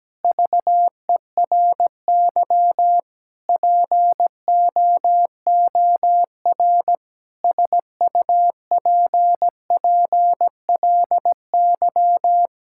Morse code, heard as a keyed tone, about 17 wpm, 700 Hz. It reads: VERY POOR SUPPLY
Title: Sentences from Top 1000 Words - Review Part 5 - 17wpm